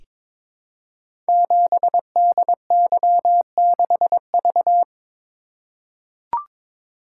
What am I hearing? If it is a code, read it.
Morse code, heard as a keyed tone, about 22 wpm, 700 Hz. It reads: 7DY6V
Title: Code Groups - 5 Letter:Numbers - 22wpm